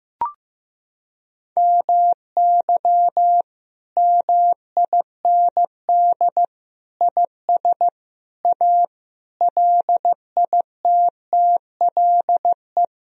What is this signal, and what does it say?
Morse code, heard as a keyed tone, about 15 wpm, 700 Hz. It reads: MY MIND IS A LITTLE
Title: Sentences from Top 1000 Words - Review Part 1 - 15wpm